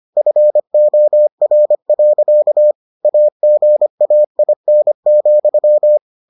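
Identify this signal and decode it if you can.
Morse code, heard as a keyed tone, about 25 wpm, 600 Hz. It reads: FOR. AGAIN,
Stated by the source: Morse edition of the Foundations of Amateur Radio episodes.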